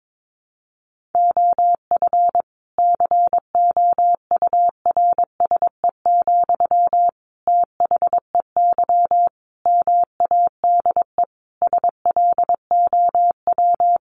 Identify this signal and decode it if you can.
Morse code, heard as a keyed tone, about 22 wpm, 700 Hz. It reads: OF COURSE, THEY MADE SLOW
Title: Sentences from Top 1400 Words 22wpm